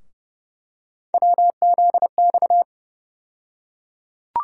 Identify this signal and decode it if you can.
Morse code, heard as a keyed tone, about 30 wpm, 700 Hz. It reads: WZX
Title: USA Call Sign Postfixes - 3 Letters 30wpm